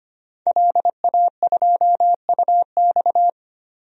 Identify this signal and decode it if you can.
Morse code, heard as a keyed tone, about 25 wpm, 700 Hz. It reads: LA2UX
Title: DXCC-100 Call Signs - 5 Characters - 25wpm